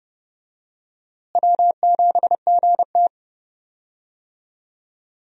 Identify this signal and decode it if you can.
Morse code, heard as a keyed tone, about 31 wpm, 700 Hz. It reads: W7GT
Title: USA Call Signs - 4 Characters - Repeated 3x - 30wpm